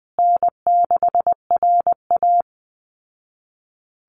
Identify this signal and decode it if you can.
Morse code, heard as a keyed tone, about 20 wpm, 700 Hz. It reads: N6RA
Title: Easy Call Signs Repeated 3x - 20wpm